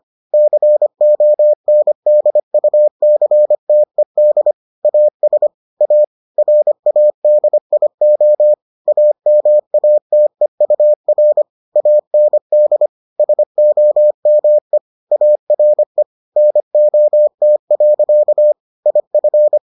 Morse code, heard as a keyed tone, about 25 wpm, 600 Hz. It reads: CONDUCTED AS A RADIO AMATEUR AND SOME ARE NOT. IF